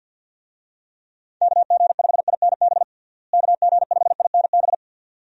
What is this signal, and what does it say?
Morse code, heard as a keyed tone, about 50 wpm, 700 Hz. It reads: KG5INB KG5INB
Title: USA Call Signs - 6 Characters - Repeated 2x - 50wpm